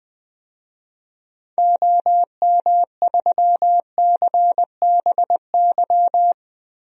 Morse code, heard as a keyed tone, about 20 wpm, 700 Hz. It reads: OM3CBY
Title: DXCC-100 Call Signs - 6 Characters - 20wpm